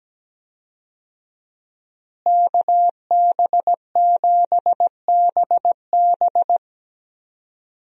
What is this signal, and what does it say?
Morse code, heard as a keyed tone, about 17 wpm, 700 Hz. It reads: KB7BB
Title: USA Call Signs - 5 Characters - Repeated 2x - 17wpm